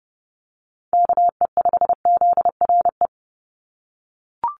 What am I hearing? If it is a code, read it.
Morse code, heard as a keyed tone, about 30 wpm, 700 Hz. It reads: KE5ZRE
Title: USA Call Signs 30wpm